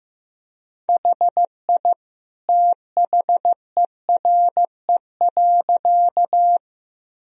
Morse code, heard as a keyed tone, about 15 wpm, 700 Hz. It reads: HI THERE.
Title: Sentences from Top 1000 Words - Review Part 3 - 15wpm